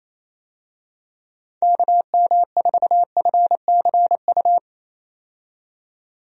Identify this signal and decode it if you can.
Morse code, heard as a keyed tone, about 28 wpm, 700 Hz. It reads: KM4FCU
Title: USA Call Signs - 6 Characters - 28wpm